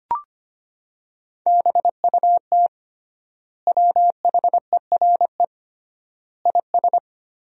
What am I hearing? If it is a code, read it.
Morse code, heard as a keyed tone, about 25 wpm, 700 Hz. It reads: BUT WHERE IS
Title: 3x Word-Spacing - Sentences from Top 200 Words - 25wpm